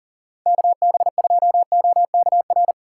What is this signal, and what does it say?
Morse code, heard as a keyed tone, about 40 wpm, 700 Hz. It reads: KB2OKR
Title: USA Callsign DE Callsign - 6 Characters - 40wpm